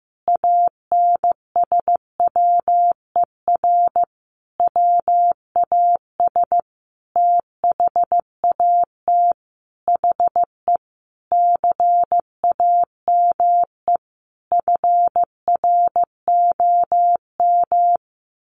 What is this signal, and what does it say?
Morse code, heard as a keyed tone, about 15 wpm, 700 Hz. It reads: ANSWER WAS THAT HE CAME FROM
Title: Sentences from Top 1500 Words 15wpm